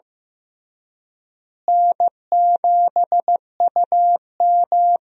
Morse code, heard as a keyed tone, about 16 wpm, 700 Hz. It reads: N7UM
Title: USA Call Signs - 4 Characters - Repeated 3x - 15wpm